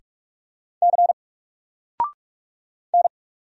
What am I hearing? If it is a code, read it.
Morse code, heard as a keyed tone, about 44 wpm, 700 Hz. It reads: C N
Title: Single Character (Letter:Number:Punctuation:Prosign) 45wpm